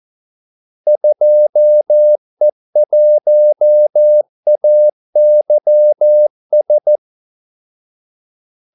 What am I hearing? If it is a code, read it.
Morse code, heard as a keyed tone, about 14 wpm, 600 Hz. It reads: 2E1AYS
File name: Callsigns-14wpm